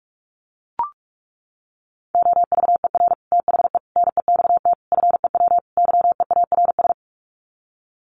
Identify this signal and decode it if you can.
Morse code, heard as a keyed tone, about 44 wpm, 700 Hz. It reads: OVER THE NEXT FEW YEARS
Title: Sets of 5 Words Encore 45wpm